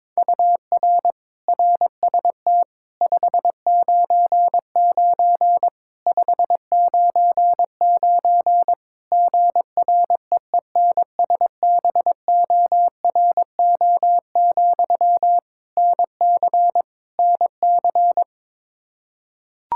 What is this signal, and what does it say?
Morse code, heard as a keyed tone, about 22 wpm, 700 Hz. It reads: UR RST 599 599 GREENSBORO, NC NC
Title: K1USN SST QSO Practice 22wpm